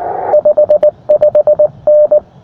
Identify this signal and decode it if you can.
Morse code, heard as a keyed tone, about 19 wpm, 600 Hz. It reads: 55N
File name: attachments / sm7dzd 20m